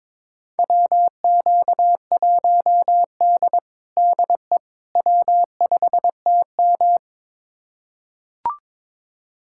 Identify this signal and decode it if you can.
Morse code, heard as a keyed tone, about 22 wpm, 700 Hz. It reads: WQ1D DE W5TM
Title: USA Callsign DE Callsign - 4 Characters - 22wpm